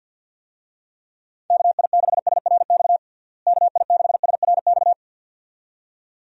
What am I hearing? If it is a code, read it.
Morse code, heard as a keyed tone, about 52 wpm, 700 Hz. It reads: KI6SRX KI6SRX
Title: USA Call Signs - 6 Characters - Repeated 2x - 50wpm